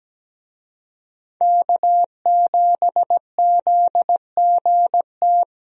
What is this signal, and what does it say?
Morse code, heard as a keyed tone, about 17 wpm, 700 Hz. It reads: K7ZGT
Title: USA Call Signs - 5 Characters - 17wpm